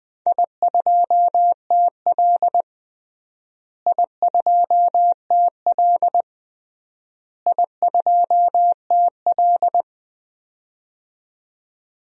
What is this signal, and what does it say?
Morse code, heard as a keyed tone, about 20 wpm, 700 Hz. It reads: I2TL I2TL I2TL
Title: DXCC-100 Call Signs - 4 Characters - Repeated 3x - 20wpm